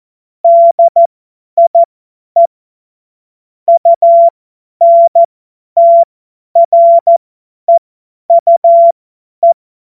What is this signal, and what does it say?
Morse code, heard as a keyed tone, about 14 wpm, spaced as for 10 wpm, 680 Hz. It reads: DIE UNTREUE
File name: Morsekurs / Witze / Witz 12 Dezember 2023 50 F70